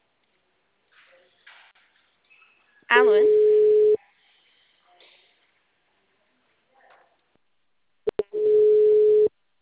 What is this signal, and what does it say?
Ringback tone (Europe, 425 Hz), with speech.